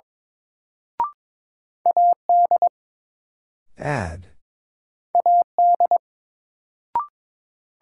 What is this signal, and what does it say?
Morse code, heard as a keyed tone, about 22 wpm, 700 Hz. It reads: AD AD
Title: Sentences from Top 1300 Words - New Word Drill Alt - 22wpm